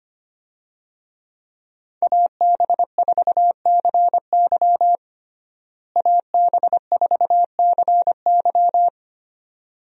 Morse code, heard as a keyed tone, about 25 wpm, 700 Hz. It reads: AB4CY AB4CY
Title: USA Call Signs - 5 Characters - Repeated 2x - 25wpm